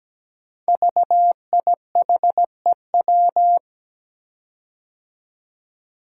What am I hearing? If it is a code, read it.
Morse code, heard as a keyed tone, about 17 wpm, 700 Hz. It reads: VIHEW
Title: Code Groups - 5 Letter:Numbers - 17wpm